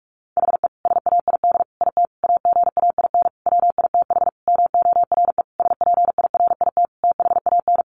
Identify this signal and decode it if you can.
Morse code, heard as a keyed tone, about 44 wpm, 700 Hz. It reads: HE SAID IT AGAIN WITH MORE SPIRIT THAN
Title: Sentences from Top 1400 Words 45wpm